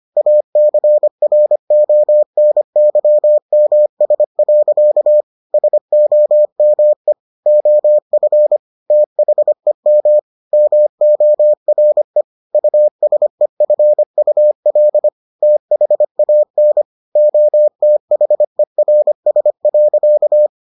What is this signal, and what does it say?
Morse code, heard as a keyed tone, about 25 wpm, 600 Hz. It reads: ACRONYMS. SOME OF THEM MORE USEFUL THAN OTHERS.